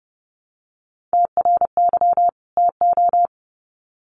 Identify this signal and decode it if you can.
Morse code, heard as a keyed tone, about 30 wpm, 700 Hz. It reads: TRY TO
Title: Sets of 2 Words 30wpm